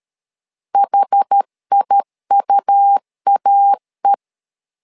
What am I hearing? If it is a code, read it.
Morse code, heard as a keyed tone, about 13 wpm, 800 Hz. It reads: HIUAE
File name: LEZIONI / 2 UHDN / 2 UH 12wpm 1439